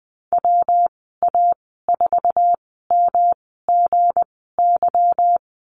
Morse code, heard as a keyed tone, about 20 wpm, 700 Hz. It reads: W A 4 M G Y
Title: 200 Lesson 32g - Letter k - Call Signs 20wpm